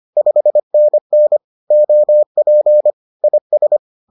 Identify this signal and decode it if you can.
Morse code, heard as a keyed tone, about 25 wpm, 600 Hz. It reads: 5NN OP IS
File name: 25wpm-6